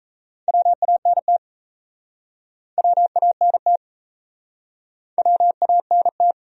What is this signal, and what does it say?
Morse code, heard as a keyed tone, about 39 wpm, 700 Hz. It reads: WANT WANT WANT